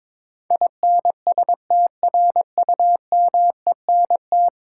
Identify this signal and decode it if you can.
Morse code, heard as a keyed tone, about 22 wpm, 700 Hz. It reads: INSTRUMENT
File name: practice / 22wpm / Ten Letter Words - 22wpm